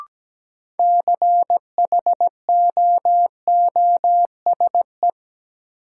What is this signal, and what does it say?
Morse code, heard as a keyed tone, about 17 wpm, 700 Hz. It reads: CHOOSE
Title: Sentences from Top 700 Words - New Word Drill - 17wpm